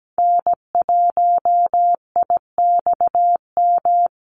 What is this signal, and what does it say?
Morse code, heard as a keyed tone, about 17 wpm, 700 Hz. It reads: N1IXM
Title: USA Call Signs 17wpm